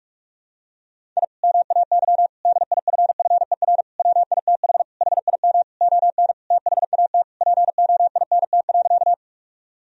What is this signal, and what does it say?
Morse code, heard as a keyed tone, about 46 wpm, 700 Hz. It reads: I MAY DIFFER WITH HIM ON THAT POINT.
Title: Sentences from Top 1000 Words - Review Part 1 - 45wpm